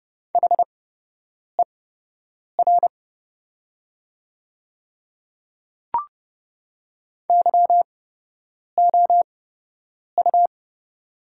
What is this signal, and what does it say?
Morse code, heard as a keyed tone, about 30 wpm, spaced as for 8 wpm, 700 Hz. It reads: HER YOU
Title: 8x Character Spacing - Top 100 Words - 30wpm